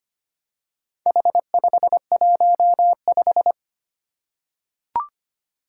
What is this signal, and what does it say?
Morse code, heard as a keyed tone, about 25 wpm, 700 Hz. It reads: H515
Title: Four Challenging Letter-Number Combinations 25wpm